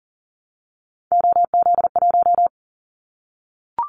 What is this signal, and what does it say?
Morse code, heard as a keyed tone, about 39 wpm, 700 Hz. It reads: OZ1
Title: Callsign Prefixes for Top 100 DXCC - 40wpm